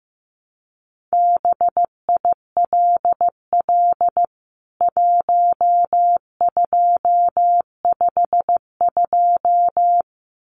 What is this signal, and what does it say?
Morse code, heard as a keyed tone, about 15 wpm, 700 Hz. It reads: BILL 1252
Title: CWT Exchange 15wpm